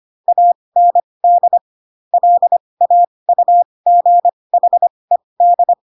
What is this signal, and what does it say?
Morse code, heard as a keyed tone, about 25 wpm, spaced as for 20 wpm, 700 Hz. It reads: AND LAUGHED
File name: Pinocchio / Pinocchio ch1 15 25 ik1hge